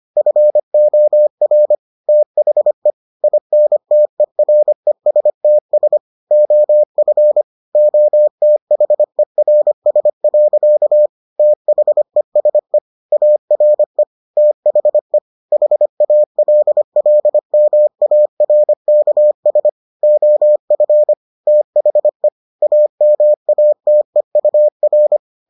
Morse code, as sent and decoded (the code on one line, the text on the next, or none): ..-. --- .-. / - .... / .. -. - . .-. ... - ... / --- ..-. / --- - .... .-. ... .-.-.- / - .... ... / .- .-. / - .... / .... .- .-.. .-.. -- .- .-. -.- ... / --- ..-. / - .... / .- -- .- - . ..- .-.
FOR THE INTERESTS OF OTHERS. THESE ARE THE HALLMARKS OF THE AMATEUR